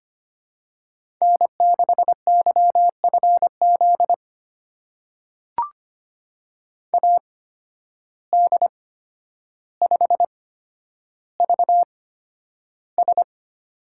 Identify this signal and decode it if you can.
Morse code, heard as a keyed tone, about 25 wpm, spaced as for 11 wpm, 700 Hz. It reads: N6YFZ AD5VS
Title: USA Call Signs - 8x Character Spacing - 25wpm